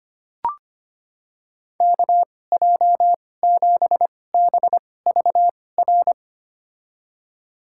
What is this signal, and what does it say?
Morse code, heard as a keyed tone, about 25 wpm, spaced as for 18 wpm, 700 Hz. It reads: KJ7BVR
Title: USA Call Signs - 6 Characters - 2x Character Spacing - 25wpm